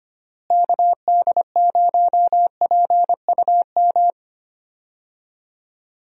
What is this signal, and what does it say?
Morse code, heard as a keyed tone, about 25 wpm, 700 Hz. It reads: KD0PUM
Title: USA Call Signs - 6 Characters - Repeated 3x - 25wpm